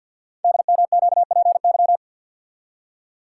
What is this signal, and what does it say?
Morse code, heard as a keyed tone, about 48 wpm, 690 Hz. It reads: DMQPY
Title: Code Groups - 5 Letters - 50wpm